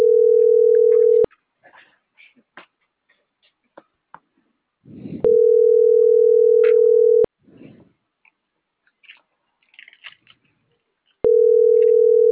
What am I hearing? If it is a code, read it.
Ringback tone (North America), 440 + 480 Hz.